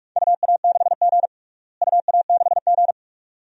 Morse code, heard as a keyed tone, about 46 wpm, 700 Hz. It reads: UA6G UA6G
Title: DXCC-100 Call Signs - 4 Characters - Repeated 2x - 45wpm